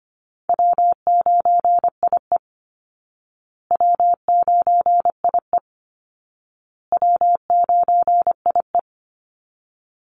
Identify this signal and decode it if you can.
Morse code, heard as a keyed tone, about 25 wpm, 700 Hz. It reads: W9IE W9IE W9IE
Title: Call Signs Repeated 2x - 25wpm